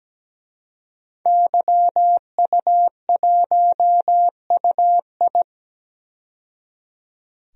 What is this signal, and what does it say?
Morse code, heard as a keyed tone, about 17 wpm, 700 Hz. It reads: YU1UI
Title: DXCC-100 Call Signs - 5 Characters - 17wpm